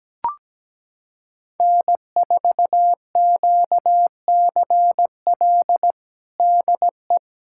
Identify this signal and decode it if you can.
Morse code, heard as a keyed tone, about 17 wpm, 700 Hz. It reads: N4QCL DE
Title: USA Callsign DE Callsign - 5 Characters - 17wpm